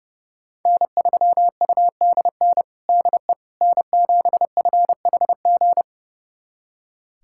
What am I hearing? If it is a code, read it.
Morse code, heard as a keyed tone, about 30 wpm, 700 Hz. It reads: N3UDN DE N7FHG
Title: USA Callsign DE Callsign - 5 Characters - 30wpm